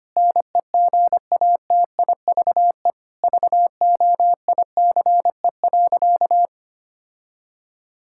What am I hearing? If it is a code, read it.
Morse code, heard as a keyed tone, about 25 wpm, 700 Hz. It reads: NEGATIVE VOICE.
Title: Sentences from Top 1000 Words - Review Part 6 - 25wpm